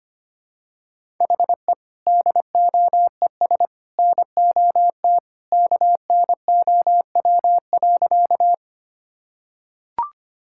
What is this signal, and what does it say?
Morse code, heard as a keyed tone, about 25 wpm, 700 Hz. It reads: HE DOES NOT KNOW.
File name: practice / 25wpm / Sentences with 3-Word Phrases - 25wpm